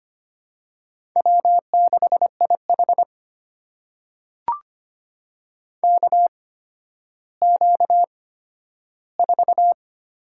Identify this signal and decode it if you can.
Morse code, heard as a keyed tone, about 25 wpm, spaced as for 12 wpm, 700 Hz. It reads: W6IH KQ4
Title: USA Call Signs - 4 Characters - 8x Character Spacing - 25wpm